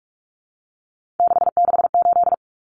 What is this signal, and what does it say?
Morse code, heard as a keyed tone, about 44 wpm, 700 Hz. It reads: 668
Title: Common Cut Numbers - 3-Digit 45wpm